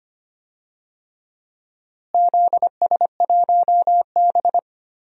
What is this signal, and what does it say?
Morse code, heard as a keyed tone, about 25 wpm, 700 Hz. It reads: ZS1B
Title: DXCC-100 Call Signs - 4 Characters - Repeated 3x - 25wpm